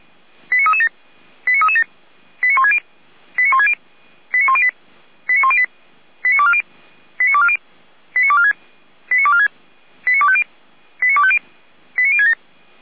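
Five-tone selective call (ZVEI), heard as quick runs of five tones, tones of about 70 ms. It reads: 89337 89337 89160 89160 89108 89108 89240 89240 89246 89246 89250 89250 89067
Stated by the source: ZVEI five-tone call received by SDR.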